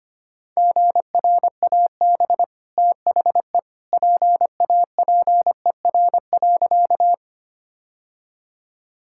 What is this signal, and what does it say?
Morse code, heard as a keyed tone, about 25 wpm, 700 Hz. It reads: GRAB THE PAPER.
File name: Easy Sentences with Hints - 25wpm